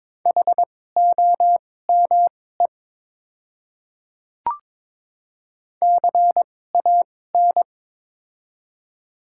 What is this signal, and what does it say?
Morse code, heard as a keyed tone, about 22 wpm, spaced as for 16 wpm, 700 Hz. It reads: HOME CAN
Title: 2x Character Spacing - Top 300 Words - 22wpm